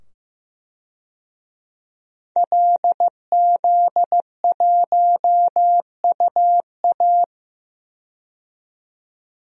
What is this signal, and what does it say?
Morse code, heard as a keyed tone, about 15 wpm, 700 Hz. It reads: LZ1UA
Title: DXCC-100 Call Signs - 5 Characters - 15wpm